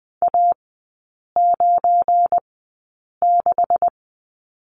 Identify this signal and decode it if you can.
Morse code, heard as a keyed tone, about 20 wpm, spaced as for 8 wpm, 700 Hz. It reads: A96
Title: 205 Lesson 33e - Letter j - Three Characters 20wpm